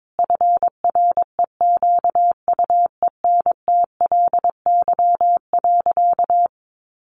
Morse code, heard as a keyed tone, about 22 wpm, 700 Hz. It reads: FREQUENTLY.
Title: Sentences from Top 1300 Words 22wpm